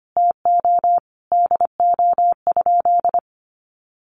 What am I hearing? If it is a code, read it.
Morse code, heard as a keyed tone, about 25 wpm, 700 Hz. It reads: TO DO?
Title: Sentences from Top 1000 Words 25wpm